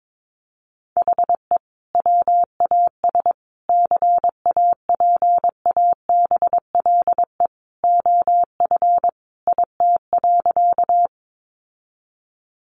Morse code, heard as a keyed tone, about 22 wpm, 700 Hz. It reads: HE WAS CAPABLE OF IT.
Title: Sentences from Top 1200 Words 22wpm